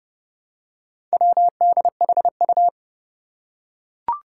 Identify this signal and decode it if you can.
Morse code, heard as a keyed tone, about 30 wpm, 700 Hz. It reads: WDHU
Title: Four Challenging Letter-Number Combinations 30wpm